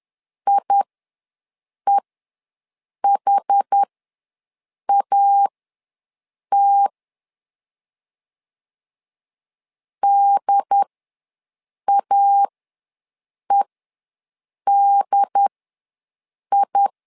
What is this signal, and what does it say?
Morse code, heard as a keyed tone, about 10 wpm, spaced as for 6 wpm, 800 Hz. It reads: IEHAT DAEDI
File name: LEZIONI / 2 UHDN / 2 UHD ws 10wpm 2058